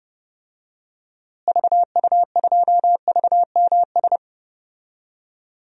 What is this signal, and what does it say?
Morse code, heard as a keyed tone, about 30 wpm, 700 Hz. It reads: VU2VMS
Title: DXCC-100 Call Signs - 6 Characters - Repeated 2x - 30wpm